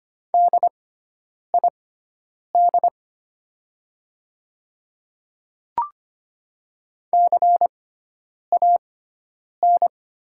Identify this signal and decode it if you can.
Morse code, heard as a keyed tone, about 25 wpm, spaced as for 9 wpm, 700 Hz. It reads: DID CAN
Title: 6x Character Spacing - Top 100 Words - 25wpm